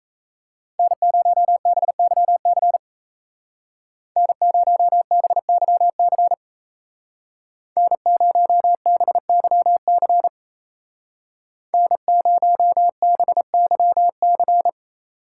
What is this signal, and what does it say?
Morse code, heard as a keyed tone, about 36 wpm, 700 Hz. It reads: N0BYC N0BYC N0BYC N0BYC